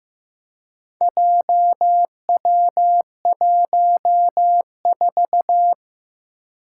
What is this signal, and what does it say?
Morse code, heard as a keyed tone, about 15 wpm, 700 Hz. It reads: JW14
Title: Four Challenging Letter-Number Combinations 15wpm